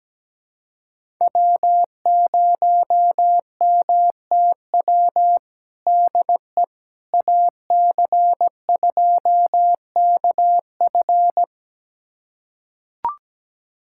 Morse code, heard as a keyed tone, about 17 wpm, 700 Hz. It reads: W0MTW DE AC2KF
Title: USA Callsign DE Callsign - 5 Characters - 17wpm